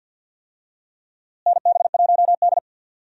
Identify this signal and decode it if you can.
Morse code, heard as a keyed tone, about 51 wpm, 700 Hz. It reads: NB1D
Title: USA Call Signs - 4 Characters - Repeated 3x - 50wpm